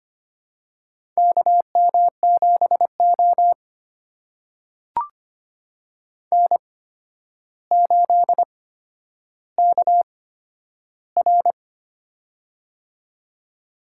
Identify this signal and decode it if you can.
Morse code, heard as a keyed tone, about 25 wpm, spaced as for 11 wpm, 700 Hz. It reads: KM7O N8KR
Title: USA Call Signs - 4 Characters - 8x Character Spacing - 25wpm